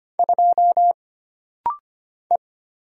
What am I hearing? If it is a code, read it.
Morse code, heard as a keyed tone, about 25 wpm, 700 Hz. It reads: 2 E
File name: Single Character (Letter:Number:Punctuation:Prosign) - Rapid-Fire - 25wpm